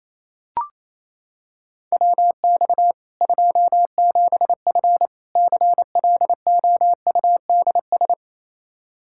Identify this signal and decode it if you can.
Morse code, heard as a keyed tone, about 28 wpm, 700 Hz. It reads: WX 27F CLOUDS
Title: K1USN SST QSO Practice 28wpm